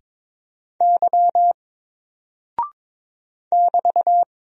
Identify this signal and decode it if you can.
Morse code, heard as a keyed tone, about 22 wpm, 700 Hz. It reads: Y <BT>
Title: Single Character (Letter:Number:Punctuation:Prosign) 22wpm